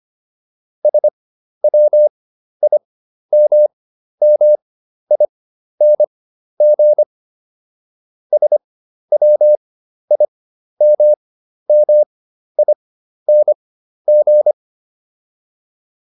Morse code, heard as a keyed tone, about 25 wpm, spaced as for 12 wpm, 600 Hz. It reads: SWIMMING SWIMMING